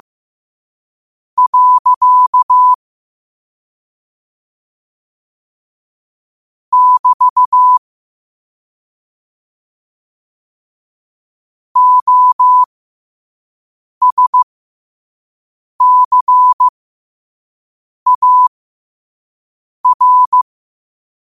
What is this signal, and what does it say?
Morse code, heard as a keyed tone, about 15 wpm, spaced as for 5 wpm, 1000 Hz. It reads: . <BT> OSCAR
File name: qotdfiles / 5wpm